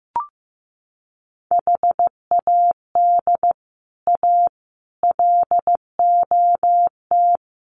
Morse code, heard as a keyed tone, about 15 wpm, 700 Hz. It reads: HAD A LOT
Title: Sets of 4 Words 15wpm